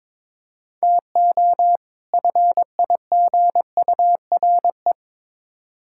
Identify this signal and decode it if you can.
Morse code, heard as a keyed tone, about 22 wpm, 700 Hz. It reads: TO FIGURE
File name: practice / 22wpm / Build A Phrase - 5-Word Phrases - 22wpm